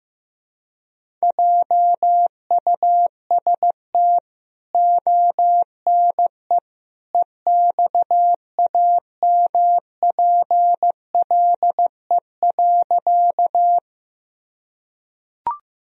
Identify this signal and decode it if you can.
Morse code, heard as a keyed tone, about 15 wpm, 700 Hz. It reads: JUST ONE EXAMPLE.
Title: Sentences from Top 1000 Words - Review Part 1 - 15wpm